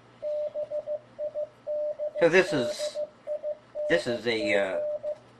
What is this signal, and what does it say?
Morse code, heard as a keyed tone, about 15 wpm, 600 Hz, with speech and radio noise. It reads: BIDDING